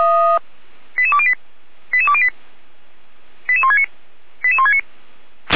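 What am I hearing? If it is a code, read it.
Five-tone selective call (ZVEI), heard as quick runs of five tones, tones of about 70 ms. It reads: 88208 88208 88169 88169
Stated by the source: ZVEI five-tone selective call received on SDR, fire brigade alerting, Feuerwehr Orlamünde.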